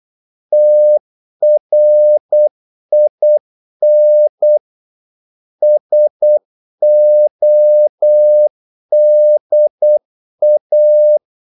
Morse code, heard as a keyed tone, about 8 wpm, 600 Hz. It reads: TRIN SODA
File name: Korte ord DK 8wpm